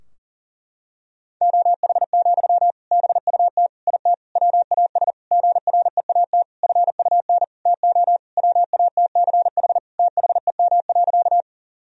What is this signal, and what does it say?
Morse code, heard as a keyed tone, about 40 wpm, 700 Hz. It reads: OH, BUT IT WAS GREAT FUN TO WATCH THEM.
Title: Sentences from Top 1000 Words - Review Part 4 - 40wpm